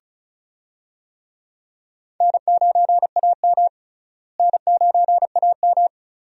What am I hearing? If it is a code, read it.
Morse code, heard as a keyed tone, about 34 wpm, 700 Hz. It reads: N9AM N9AM
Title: USA Call Signs - 4 Characters - Repeated 2x - 35wpm